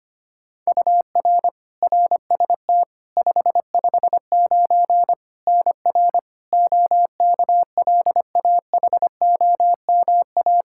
Morse code, heard as a keyed tone, about 25 wpm, 700 Hz. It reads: UR RST 559 NR OKLAHOMA
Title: Practice QSO - Exchange 1 - Beginning - 25wpm